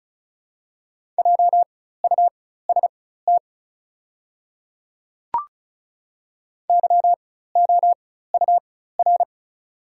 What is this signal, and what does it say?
Morse code, heard as a keyed tone, about 36 wpm, spaced as for 16 wpm, 690 Hz. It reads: JUST YOUR
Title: 4x Character Spacing - Top 300 Words - 35wpm